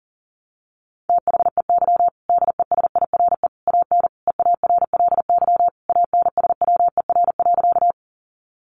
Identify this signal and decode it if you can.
Morse code, heard as a keyed tone, about 37 wpm, 700 Hz. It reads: THEY DESIRE AN EARLY ANSWER.
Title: Sentences from Top 1100 Words 40wpm